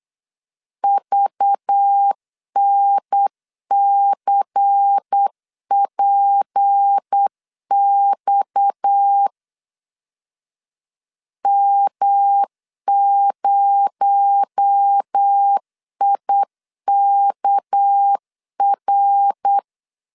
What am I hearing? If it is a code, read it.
Morse code, heard as a keyed tone, about 8 wpm, 800 Hz. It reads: VNCPX M0IKR
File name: LEZIONI / 6 PKX0 / 6 PKX0 8wpm 3341